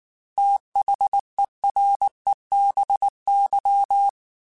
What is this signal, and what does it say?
Morse code, heard as a keyed tone, about 19 wpm, 800 Hz. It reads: THEREBY